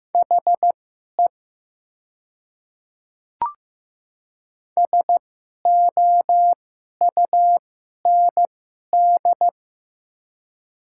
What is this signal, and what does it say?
Morse code, heard as a keyed tone, about 15 wpm, spaced as for 11 wpm, 700 Hz. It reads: HE SOUND
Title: 2x Character Spacing - Top 100 Words - 15wpm